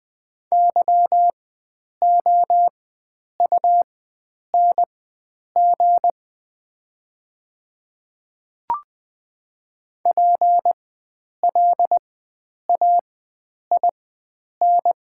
Morse code, heard as a keyed tone, about 20 wpm, spaced as for 9 wpm, 700 Hz. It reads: YOUNG PLAIN
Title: 4x Character Spacing - Top 500 Words - 20wpm